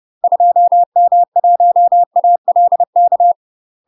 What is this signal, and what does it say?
Morse code, heard as a keyed tone, about 30 wpm, 700 Hz. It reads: 2M1ALK
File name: UK MASTER QSOs TRAINING/30WPM